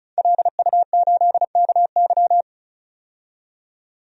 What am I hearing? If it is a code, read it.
Morse code, heard as a keyed tone, about 36 wpm, 700 Hz. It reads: LU8KY
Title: DXCC-100 Call Signs - 5 Characters - 35wpm